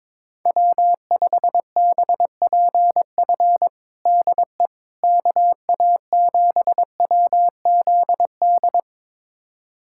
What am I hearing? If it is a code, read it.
Morse code, heard as a keyed tone, about 22 wpm, 700 Hz. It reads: W5BPF DE KA7WZD
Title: USA Callsign DE Callsign 22wpm